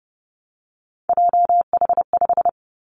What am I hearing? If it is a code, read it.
Morse code, heard as a keyed tone, about 30 wpm, 700 Hz. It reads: JH5
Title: Three Challenging Letter-Number Combinations 30wpm